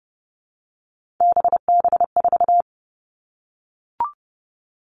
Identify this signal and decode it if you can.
Morse code, heard as a keyed tone, about 30 wpm, 700 Hz. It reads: BB4
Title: All Cut Numbers - 3-Digit 30wpm